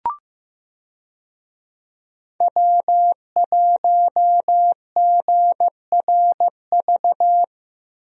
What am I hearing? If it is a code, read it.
Morse code, heard as a keyed tone, about 15 wpm, 700 Hz. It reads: W1GRV
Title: USA Call Signs - 5 Characters - Repeated 2x - 15wpm